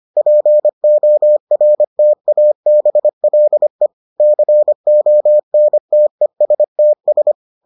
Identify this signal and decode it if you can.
Morse code, heard as a keyed tone, about 25 wpm, 600 Hz. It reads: PORTABLE CONTESTS